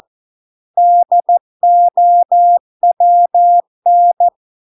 Morse code, heard as a keyed tone, about 14 wpm, 700 Hz. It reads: DOWN